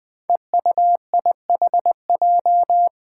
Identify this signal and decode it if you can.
Morse code, heard as a keyed tone, about 20 wpm, 700 Hz. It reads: EUIHJ
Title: Five Challenging Letter-Number Combinations 20wpm